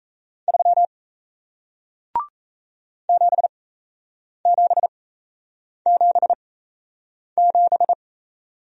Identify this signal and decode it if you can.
Morse code, heard as a keyed tone, about 38 wpm, spaced as for 8 wpm, 700 Hz. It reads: 3 7777